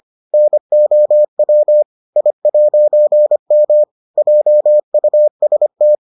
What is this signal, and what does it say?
Morse code, heard as a keyed tone, about 25 wpm, 600 Hz. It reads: NOW I'M JUST